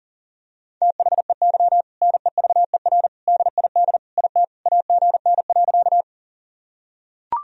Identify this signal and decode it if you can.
Morse code, heard as a keyed tone, about 40 wpm, 700 Hz. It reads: THEY NEVER DID IT AGN.
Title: Sentences from Top 300 Words and Abbreviations 40wpm